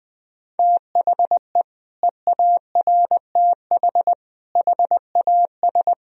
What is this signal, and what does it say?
Morse code, heard as a keyed tone, about 19 wpm, 700 Hz. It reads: THE EARTH HAS
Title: Sentences from Top 1000 Words - Review Part 2 - 20wpm